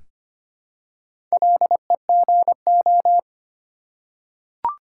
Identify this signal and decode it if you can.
Morse code, heard as a keyed tone, about 25 wpm, 700 Hz. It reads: LEGO
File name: practice / 25wpm / Difficult Words with Hints - 25wpm